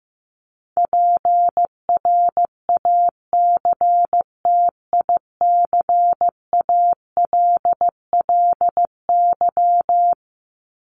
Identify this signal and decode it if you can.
Morse code, heard as a keyed tone, about 15 wpm, 700 Hz. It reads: PRACTICALLY
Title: LY Suffix - Top 100 Words with Base Words 15wpm